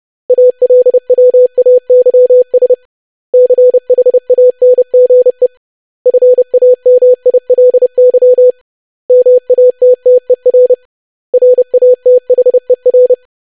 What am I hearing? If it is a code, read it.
Morse code, heard as a keyed tone, about 30 wpm, 500 Hz. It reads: ALWAYS CHANGE FAMILY MATTER RATHER
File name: lesson / web / basic500-25wpm